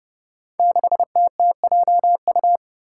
Morse code, heard as a keyed tone, about 30 wpm, 700 Hz. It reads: 6TTJU
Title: Five Challenging Letter-Number Combinations 30wpm